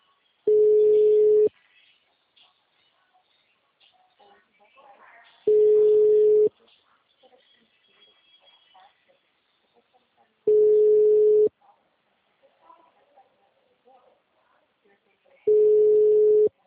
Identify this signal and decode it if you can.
Ringback tone (Europe, 425 Hz), with speech.